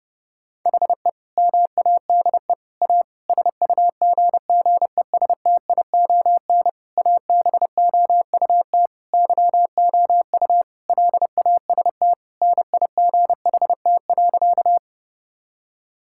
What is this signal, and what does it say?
Morse code, heard as a keyed tone, about 30 wpm, 700 Hz. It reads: HE MADE A SUGGESTION ABOUT YOU LAST NIGHT.
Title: Sentences from Top 1600 Words 30wpm